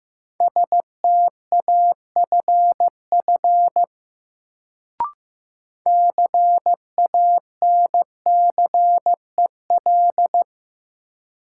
Morse code, heard as a keyed tone, about 15 wpm, 700 Hz. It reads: STAFF CANCEL
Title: Sentences from Top 1200 Words - New Word Drill - 15wpm